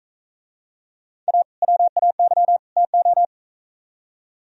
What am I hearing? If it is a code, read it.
Morse code, heard as a keyed tone, about 42 wpm, 700 Hz. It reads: A WAY TO